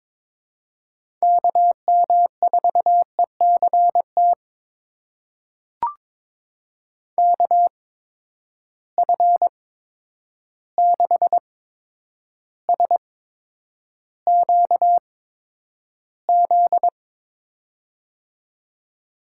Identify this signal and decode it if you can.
Morse code, heard as a keyed tone, about 22 wpm, spaced as for 9 wpm, 700 Hz. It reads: KM4ECT KF6SQZ
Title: USA Call Signs - 6 Characters - 8x Character Spacing - 22wpm